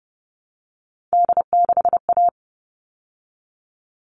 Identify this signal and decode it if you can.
Morse code, heard as a keyed tone, about 30 wpm, 700 Hz. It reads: D6A
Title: All Cut Numbers - 3-Digit 30wpm